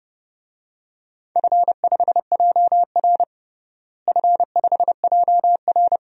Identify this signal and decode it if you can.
Morse code, heard as a keyed tone, about 32 wpm, 700 Hz. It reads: F5JR F5JR
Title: DXCC-100 Call Signs - 4 Characters - Repeated 3x - 30wpm